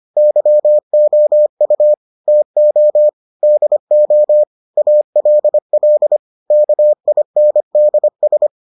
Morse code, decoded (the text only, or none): YOU TO DO ALL KINDS